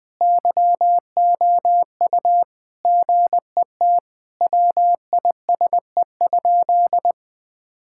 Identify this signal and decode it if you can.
Morse code, heard as a keyed tone, about 21 wpm, 700 Hz. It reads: YOU GET WISE?
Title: Sentences from Top 1600 Words 20wpm